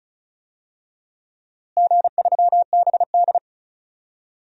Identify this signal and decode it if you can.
Morse code, heard as a keyed tone, about 36 wpm, 700 Hz. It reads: G3BD
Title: DXCC-100 Call Signs - 4 Characters - Repeated 2x - 35wpm